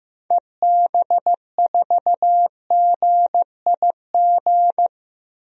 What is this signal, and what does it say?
Morse code, heard as a keyed tone, about 16 wpm, 700 Hz. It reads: EB4GIG
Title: DXCC-100 Call Signs - 6 Characters - Repeated 2x - 15wpm